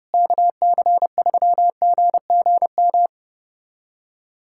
Morse code, heard as a keyed tone, about 30 wpm, 700 Hz. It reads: KC3GGM
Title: USA Callsign DE Callsign - 6 Characters - 30wpm